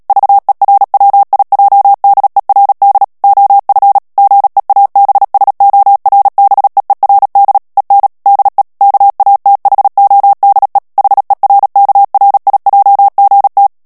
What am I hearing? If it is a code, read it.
Morse code, heard as a keyed tone, about 37 wpm, 800 Hz. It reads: VERWIJDERD OF GEABSORBEERD EN DE KATHODE HERKRIJGT
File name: morse mp3 / HOLLANDAIS texte clair 2 vitesse 35 mots minute